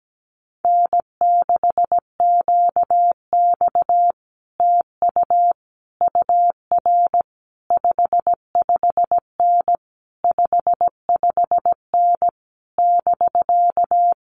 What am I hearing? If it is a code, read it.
Morse code, heard as a keyed tone, about 17 wpm, 700 Hz. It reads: N6QX TU UR 55N 55N <BK>
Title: Common QSO Phrases Encore 17wpm